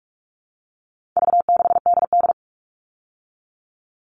Morse code, heard as a keyed tone, about 44 wpm, 700 Hz. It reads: V6DD
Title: All Cut Numbers - 4-Digit 45wpm